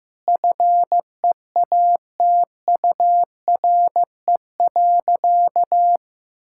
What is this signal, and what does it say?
Morse code, heard as a keyed tone, about 16 wpm, 700 Hz. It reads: FEATURE.
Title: Sentences from Top 1000 Words - Review Part 5 - 15wpm